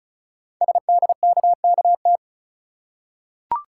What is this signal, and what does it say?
Morse code, heard as a keyed tone, about 36 wpm, 690 Hz. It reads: SDKKT
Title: Code Groups - 5 Letters - 35wpm